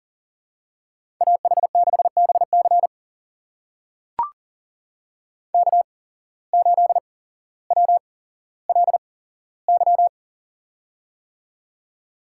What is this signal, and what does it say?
Morse code, heard as a keyed tone, about 40 wpm, spaced as for 17 wpm, 700 Hz. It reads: AH6BC K8WLY
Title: USA Call Signs - 5 Characters - 8x Character Spacing - 40wpm